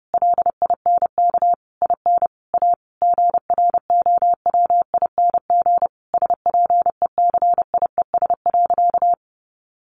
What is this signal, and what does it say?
Morse code, heard as a keyed tone, about 30 wpm, 700 Hz. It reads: LINK IN A GROWING SPECIES.
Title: A Sound Of Thunder 30wpm